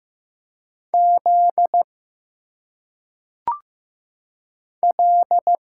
Morse code, heard as a keyed tone, about 15 wpm, 700 Hz. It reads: Z L
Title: Single Letters - Phonetic Pronunciation 15wpm